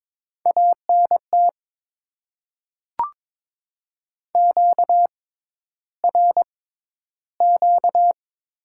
Morse code, heard as a keyed tone, about 22 wpm, spaced as for 11 wpm, 700 Hz. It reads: ANT QRQ
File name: Top 90 QSO Elements - 6x Character Spacing - 22wpm